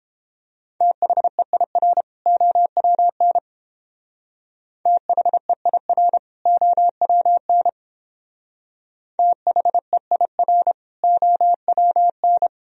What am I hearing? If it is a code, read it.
Morse code, heard as a keyed tone, about 30 wpm, 700 Hz. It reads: THEIR OWN THEIR OWN THEIR OWN